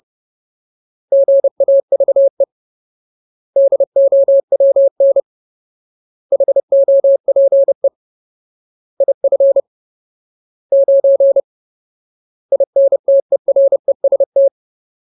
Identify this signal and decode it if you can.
Morse code, heard as a keyed tone, about 30 wpm, 550 Hz. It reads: GAVE DOWN HOPE IF 9 INTEREST
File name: Ultimate Word List / 30WPM Ultimate